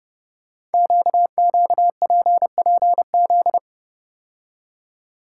Morse code, heard as a keyed tone, about 29 wpm, 700 Hz. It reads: QQPPZ
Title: Code Groups - 5 Letters - 30wpm